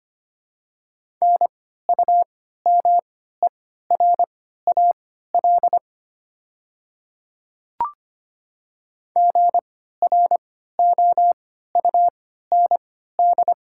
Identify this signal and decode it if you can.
Morse code, heard as a keyed tone, about 25 wpm, spaced as for 14 wpm, 690 Hz. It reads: NUMERAL GROUND
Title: 3x Character Spacing - Top 500 Words - 25wpm